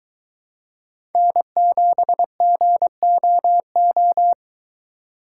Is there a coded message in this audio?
Yes: Morse code from 1.1 to 4.3 s, heard as a keyed tone, about 23 wpm, 700 Hz. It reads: N7GOO